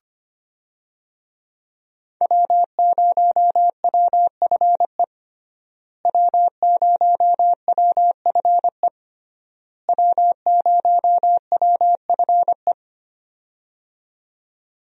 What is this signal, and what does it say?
Morse code, heard as a keyed tone, about 25 wpm, 700 Hz. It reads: W0WFE W0WFE W0WFE
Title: USA Call Signs - 5 Characters - Repeated 3x - 25wpm